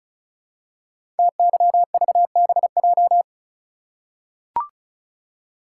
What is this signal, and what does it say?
Morse code, heard as a keyed tone, about 36 wpm, 700 Hz. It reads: TYVBJ
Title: Code Groups - 5 Letters - 35wpm